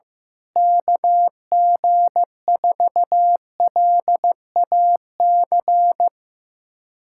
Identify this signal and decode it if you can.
Morse code, heard as a keyed tone, about 15 wpm, 700 Hz. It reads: KG4LAC
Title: USA Callsign DE Callsign - 6 Characters - 15wpm